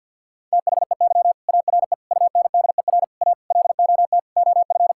Morse code, heard as a keyed tone, about 49 wpm, 700 Hz. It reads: THEY ARE UNDER A LOT OF
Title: Sentences from Top 1000 Words - Review Part 6 - 50wpm